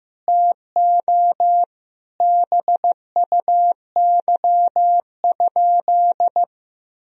Morse code, heard as a keyed tone, about 15 wpm, 700 Hz. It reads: TO BUY?
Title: Sentences from Top 1000 Words - Review Part 4 - 15wpm